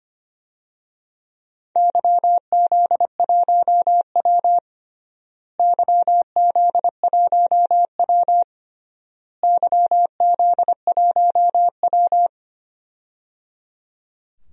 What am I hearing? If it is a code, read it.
Morse code, heard as a keyed tone, about 25 wpm, 700 Hz. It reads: YZ1W YZ1W YZ1W
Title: DXCC-100 Call Signs - 4 Characters - Repeated 3x - 25wpm